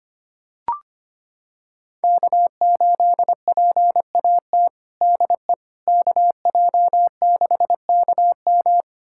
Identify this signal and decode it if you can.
Morse code, heard as a keyed tone, about 25 wpm, 700 Hz. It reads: K8PAT DE KJ6KM
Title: USA Callsign DE Callsign - 5 Characters - 25wpm